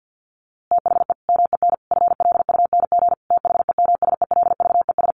Morse code, heard as a keyed tone, about 50 wpm, 700 Hz. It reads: THE MEN FLUNG THEMSELVES
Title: A Sound Of Thunder 50wpm